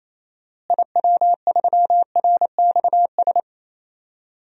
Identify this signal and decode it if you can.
Morse code, heard as a keyed tone, about 28 wpm, 700 Hz. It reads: IW3RXS
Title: DXCC-100 Call Signs - 6 Characters - Repeated 3x - 28wpm